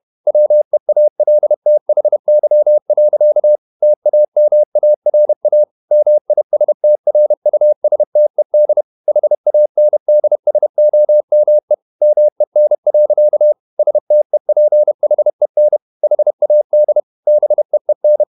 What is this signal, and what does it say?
Morse code, heard as a keyed tone, about 31 wpm, 600 Hz. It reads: WEALTHY. TAMARA MISTRUSTED HANDSOME MEN. STEPHEN HAD BEEN
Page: Never 31wpm | CW med Gnister